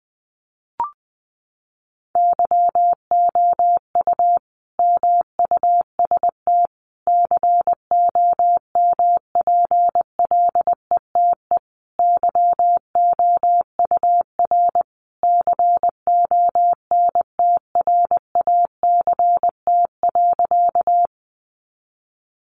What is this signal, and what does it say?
Morse code, heard as a keyed tone, about 20 wpm, 700 Hz. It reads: YOU MUST COMPLETE YOUR CONTRACT.
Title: Sentences from Top 1400 Words 20wpm